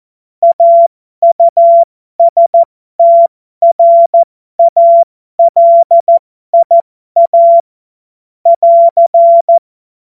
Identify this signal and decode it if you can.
Morse code, heard as a keyed tone, about 14 wpm, spaced as for 12 wpm, 680 Hz. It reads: AUSTRALIA <AR>
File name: Morsekurs / Witze